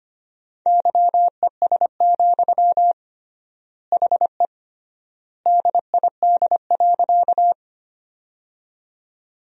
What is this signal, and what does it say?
Morse code, heard as a keyed tone, about 25 wpm, 700 Hz. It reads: YES, HE DID.
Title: 3x Word-Spacing - Sentences from Top 500 Words - 25wpm